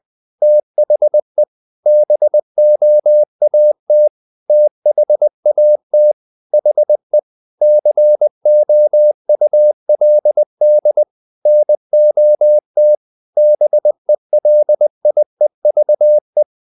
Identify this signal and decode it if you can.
Morse code, heard as a keyed tone, about 20 wpm, 600 Hz. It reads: THE BOAT THAT HE COULD NOT BELIEVE
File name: the old man and the sea part 2 20wpm